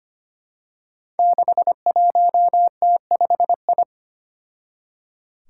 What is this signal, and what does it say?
Morse code, heard as a keyed tone, about 25 wpm, 700 Hz. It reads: 61T5I
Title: Five Challenging Letter-Number Combinations 25wpm